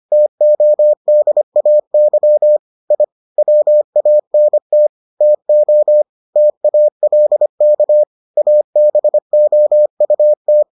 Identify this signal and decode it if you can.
Morse code, heard as a keyed tone, about 25 wpm, 600 Hz. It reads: TODAY I WANT TO TALK ABOUT